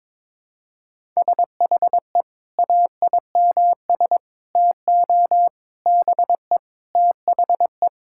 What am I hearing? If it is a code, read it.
Morse code, heard as a keyed tone, about 21 wpm, 700 Hz. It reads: SHE AIMS TO BE THE
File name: Sentences with 9-Word Phrases - 22wpm